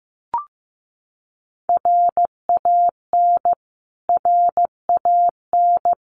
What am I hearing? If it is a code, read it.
Morse code, heard as a keyed tone, about 15 wpm, 700 Hz. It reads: RAN RAN
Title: Top 500 Words Repeated 15wpm